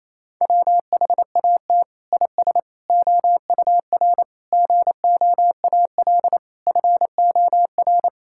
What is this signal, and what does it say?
Morse code, heard as a keyed tone, about 28 wpm, 700 Hz. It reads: WHAT IS OUR GOAL FOR
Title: Sentences from Top 1000 Words - Review Part 5 - 28wpm